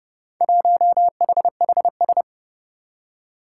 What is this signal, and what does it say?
Morse code, heard as a keyed tone, about 28 wpm, 700 Hz. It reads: 1HHS
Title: Four Challenging Letter-Number Combinations 30wpm